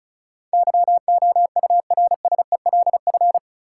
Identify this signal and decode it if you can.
Morse code, heard as a keyed tone, about 36 wpm, 700 Hz. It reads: YOURSELF
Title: Sentences from Top 600 Words - New Word Drill - 35wpm